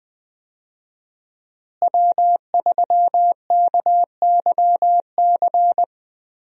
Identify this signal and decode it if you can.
Morse code, heard as a keyed tone, about 20 wpm, 700 Hz. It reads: W3KYC
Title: USA Call Signs - 5 Characters - Repeated 2x - 20wpm